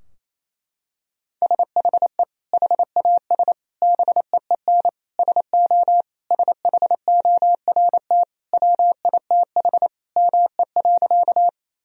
Morse code, heard as a keyed tone, about 28 wpm, 700 Hz. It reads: SHE HAS BEEN SO SHORT WITH ME.
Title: Sentences from Top 1000 Words - Review Part 2 - 28wpm